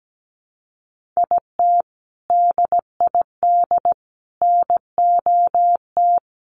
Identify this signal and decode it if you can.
Morse code, heard as a keyed tone, about 17 wpm, 700 Hz. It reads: IT DID NOT
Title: Sets of 3 Words 17wpm